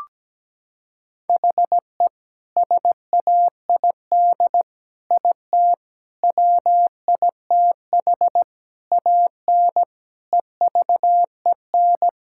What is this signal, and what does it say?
Morse code, heard as a keyed tone, about 17 wpm, 700 Hz. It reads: HE SAID IT WITH AN EVEN
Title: Sentences from Top 1600 Words 17wpm